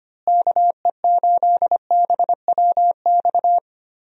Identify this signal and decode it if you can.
Morse code, heard as a keyed tone, about 25 wpm, 700 Hz. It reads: KE8BWX
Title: USA Call Signs - 6 Characters - Repeated 3x - 25wpm